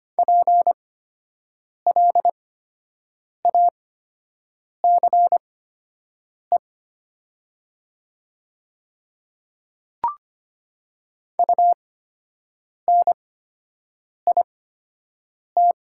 Morse code, heard as a keyed tone, about 25 wpm, spaced as for 7 wpm, 700 Hz. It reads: PLACE UNIT
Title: 8x Character Spacing - Top 500 Words - 25wpm